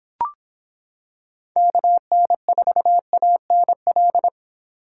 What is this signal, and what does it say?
Morse code, heard as a keyed tone, about 26 wpm, 700 Hz. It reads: KN4ANL